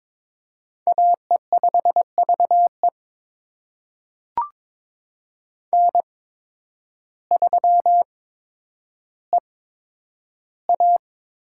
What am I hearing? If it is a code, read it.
Morse code, heard as a keyed tone, about 22 wpm, spaced as for 10 wpm, 700 Hz. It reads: AE5VE N3EA
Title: USA Call Signs - 5 Characters - 8x Character Spacing - 22wpm